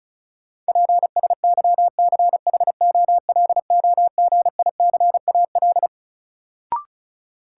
Morse code, heard as a keyed tone, about 34 wpm, 700 Hz. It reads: PSYCHOLOGICAL
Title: Sentences from Top 1400 Words - New Word Drill - 35wpm